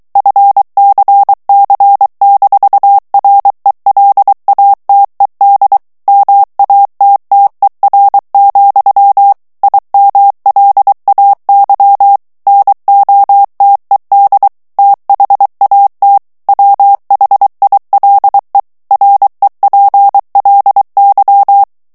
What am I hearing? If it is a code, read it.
Morse code, heard as a keyed tone, about 23 wpm, 800 Hz. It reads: FCC-RELATED MATTER, IMLAY NOTED THAT WHILE REPLY
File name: ANGLAIS texte clair 3 vitesse 22 mots minute